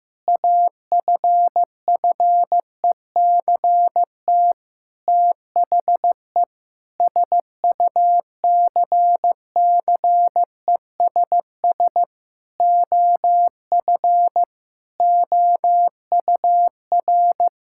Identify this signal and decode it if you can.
Morse code, heard as a keyed tone, about 15 wpm, 700 Hz. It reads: AFFECT THE SUCCESS OF OUR
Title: Sentences from Top 1000 Words - Review Part 6 - 15wpm